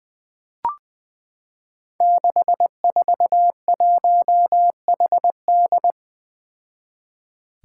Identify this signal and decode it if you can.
Morse code, heard as a keyed tone, about 21 wpm, 700 Hz. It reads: 641HD
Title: Five Challenging Letter-Number Combinations 20wpm